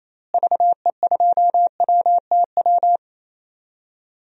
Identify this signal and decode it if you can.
Morse code, heard as a keyed tone, about 28 wpm, 700 Hz. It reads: VE2WTW
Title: DXCC-100 Call Signs - 6 Characters - Repeated 2x - 28wpm